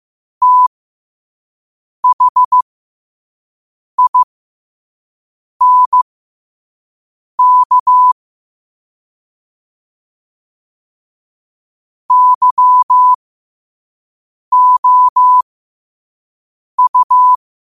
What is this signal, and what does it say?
Morse code, heard as a keyed tone, about 15 wpm, spaced as for 5 wpm, 1000 Hz. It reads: THINK YOU
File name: qotdfiles / 5wpm